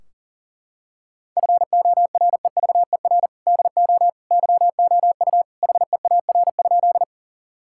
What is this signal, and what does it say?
Morse code, heard as a keyed tone, about 40 wpm, 700 Hz. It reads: FOREVER DO YOU HEAR?
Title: Sentences from Top 1600 Words 40wpm